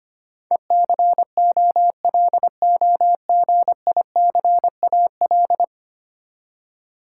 Morse code, heard as a keyed tone, about 25 wpm, 700 Hz. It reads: ECOLOGICAL
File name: Ten Letter Words - 25wpm